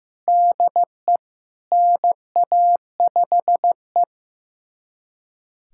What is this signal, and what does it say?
Morse code, heard as a keyed tone, about 16 wpm, 700 Hz. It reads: DE NA5E
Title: USA Callsign DE Callsign - 4 Characters - 15wpm